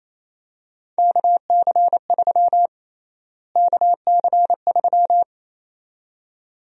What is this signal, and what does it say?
Morse code, heard as a keyed tone, about 27 wpm, 700 Hz. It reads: KC3 KC3
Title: USA Call Sign Prefixes - Repeated 2x - 28wpm